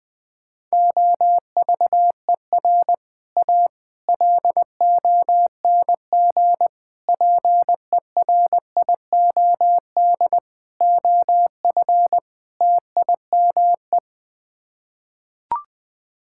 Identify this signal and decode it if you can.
Morse code, heard as a keyed tone, about 20 wpm, 700 Hz. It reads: OVER A LONG PERIOD OF TIME
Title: Sets of 6 Words - 20wpm